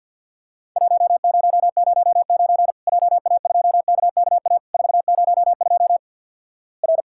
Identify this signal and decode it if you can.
Morse code, heard as a keyed tone, about 50 wpm, 700 Hz. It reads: 1009 JA2KKA 402 R
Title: Worked All Europe DX Contest - QTC Exchange with Repeats - 50wpm